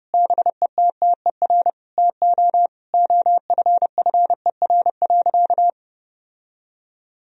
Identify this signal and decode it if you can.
Morse code, heard as a keyed tone, about 32 wpm, 700 Hz. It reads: BETTER TO OFFER.
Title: Sentences from Top 1000 Words - Review Part 4 - 30wpm